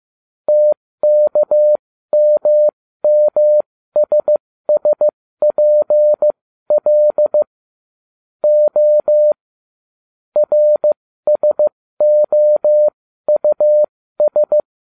Morse code, heard as a keyed tone, about 15 wpm, spaced as for 13 wpm, 600 Hz. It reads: TKMMSSPL O RSOUS